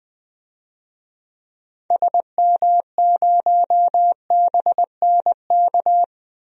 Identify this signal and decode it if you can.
Morse code, heard as a keyed tone, about 20 wpm, 700 Hz. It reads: SM0BNK
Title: DXCC-100 Call Signs - 6 Characters - Repeated 3x - 20wpm